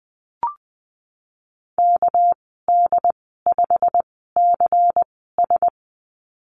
Morse code, heard as a keyed tone, about 20 wpm, spaced as for 14 wpm, 700 Hz. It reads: KD5CS
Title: 213 Lesson 34g - Number 8 - Call Signs 20wpm